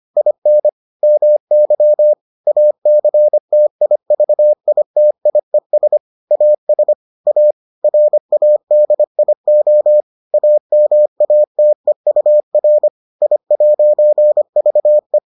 Morse code, heard as a keyed tone, about 25 wpm, 600 Hz. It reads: IN MY ACTIVITIES AS A RADIO AMATEUR I'VE